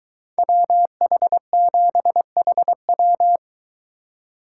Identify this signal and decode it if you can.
Morse code, heard as a keyed tone, about 23 wpm, 700 Hz. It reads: WH7HW